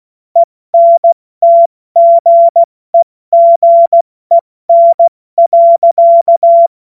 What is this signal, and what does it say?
Morse code, heard as a keyed tone, about 16 wpm, 680 Hz. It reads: ENTGEGEN.
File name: Morsekurs / Witze / Witz 13 Mai 2025 70 F80